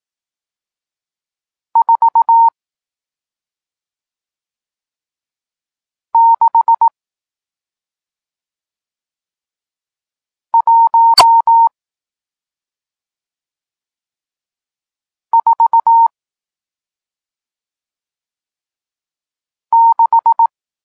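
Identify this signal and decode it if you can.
Morse code, heard as a keyed tone, about 18 wpm, spaced as for 2 wpm, 920 Hz. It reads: 46146
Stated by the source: Morse code lesson teaching digits 1, 4 and 6, from SP9MOA, Niepołomicki Klub Krótkofalowców (Poland) amateur radio club.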